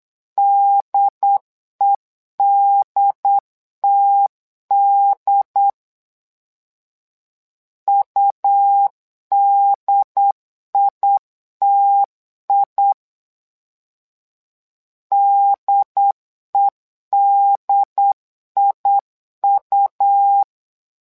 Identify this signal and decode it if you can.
Morse code, heard as a keyed tone, about 8 wpm, 800 Hz. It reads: DEDTD UDITI DEDIU
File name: LEZIONI / 2 UHDN / 2 UHD 8wpm 2258